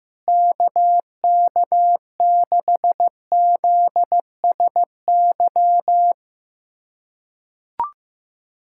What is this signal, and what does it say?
Morse code, heard as a keyed tone, about 15 wpm, 700 Hz. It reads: KK6ZSY
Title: USA Callsign DE Callsign 15wpm